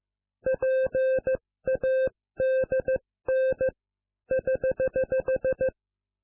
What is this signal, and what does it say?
Morse code, heard as a keyed tone, about 15 wpm, 520 Hz. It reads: PADN <HH>